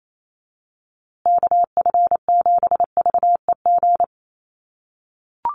Morse code, heard as a keyed tone, about 28 wpm, 700 Hz. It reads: KF7VEG
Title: USA Call Signs Repeated 2x 28wpm